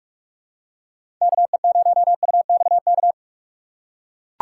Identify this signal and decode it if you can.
Morse code, heard as a keyed tone, about 42 wpm, 700 Hz. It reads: KE0UXK
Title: USA Call Signs - 6 Characters - 4x Character Spacing - 45wpm